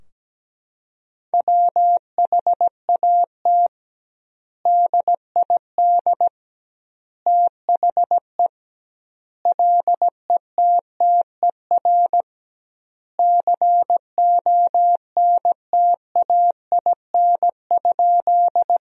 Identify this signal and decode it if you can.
Morse code, heard as a keyed tone, about 17 wpm, 700 Hz. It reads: WHAT DID THE LETTER CONTAIN?
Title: 2x Word-Spacing - Sentences from Top 500 Words - 17wpm